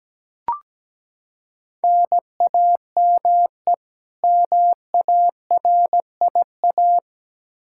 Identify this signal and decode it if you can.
Morse code, heard as a keyed tone, about 17 wpm, 700 Hz. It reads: NAME MARIA
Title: Practice QSO - Exchange 1 - Beginning - 17wpm